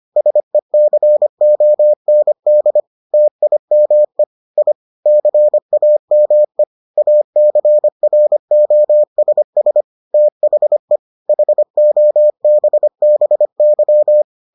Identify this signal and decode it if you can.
Morse code, heard as a keyed tone, about 25 wpm, 600 Hz. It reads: SECOND TIME I CAME ACROSS THE HOBBY